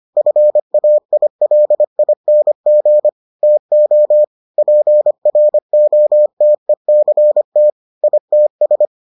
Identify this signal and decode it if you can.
Morse code, heard as a keyed tone, about 25 wpm, 600 Hz. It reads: FAILING TO PROTECT ITS